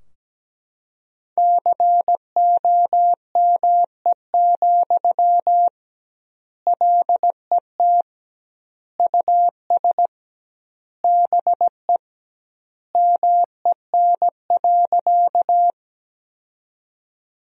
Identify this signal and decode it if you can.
Morse code, heard as a keyed tone, about 17 wpm, 700 Hz. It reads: COME, LET US BE MEN.
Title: 2x Word-Spacing - Sentences from Top 300 Words - 17wpm